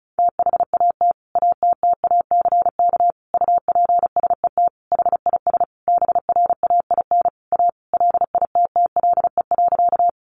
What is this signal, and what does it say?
Morse code, heard as a keyed tone, about 34 wpm, 700 Hz. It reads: THAT ATTACK UPSET HIS BRAIN A LITTLE.
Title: Sentences from Top 1500 Words 35wpm